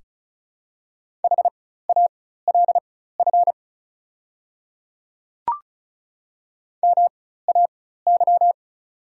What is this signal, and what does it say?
Morse code, heard as a keyed tone, about 36 wpm, spaced as for 16 wpm, 690 Hz. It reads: HALF MAY
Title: 4x Character Spacing - Top 500 Words - 35wpm